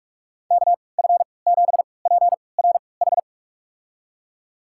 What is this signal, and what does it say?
Morse code, heard as a keyed tone, about 45 wpm, spaced as for 25 wpm, 700 Hz. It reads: KF7PRH
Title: USA Call Signs - 6 Characters - 3x Character Spacing - 45wpm